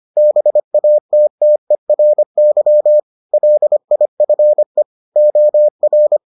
Morse code, heard as a keyed tone, about 25 wpm, 600 Hz. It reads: BATTERY LIFE OR